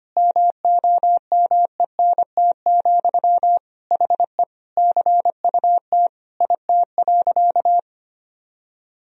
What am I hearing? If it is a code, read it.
Morse code, heard as a keyed tone, about 25 wpm, 700 Hz. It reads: MOMENT, HE CUT IT.
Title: Sentences from Top 1000 Words - Review Part 1 - 25wpm